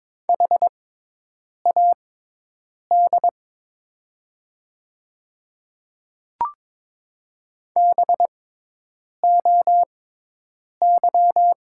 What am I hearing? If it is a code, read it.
Morse code, heard as a keyed tone, about 22 wpm, spaced as for 8 wpm, 700 Hz. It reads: HAD BOY
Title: 6x Character Spacing - Top 200 Words - 22wpm